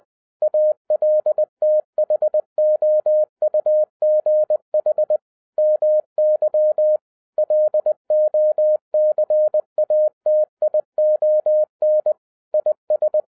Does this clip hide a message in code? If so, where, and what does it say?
Yes: Morse code from 0.4 to 13.2 s, heard as a keyed tone, about 20 wpm, 610 Hz. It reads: ALTHOUGH MY LOCATION IS